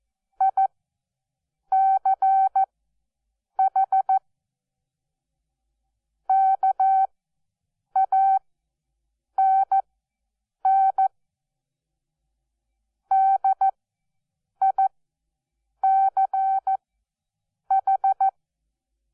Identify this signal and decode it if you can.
Morse code, heard as a keyed tone, about 14 wpm, spaced as for 7 wpm, 770 Hz. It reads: ICH KANN DICH
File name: Onlinelehrgang / morsen / DJ4UF Morsen12